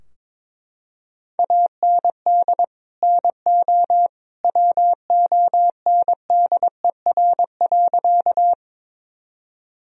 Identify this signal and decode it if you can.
Morse code, heard as a keyed tone, about 22 wpm, 700 Hz. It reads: AND NO WONDER.
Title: Sentences from Top 1000 Words - Review Part 3 - 22wpm